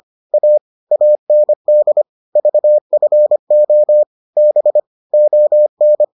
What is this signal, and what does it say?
Morse code, heard as a keyed tone, about 25 wpm, 600 Hz. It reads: A AND VFO B ON